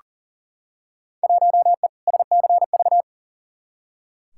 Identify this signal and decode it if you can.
Morse code, heard as a keyed tone, about 40 wpm, 700 Hz. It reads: 1E SCV
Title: Field Day Exchanges - 40wpm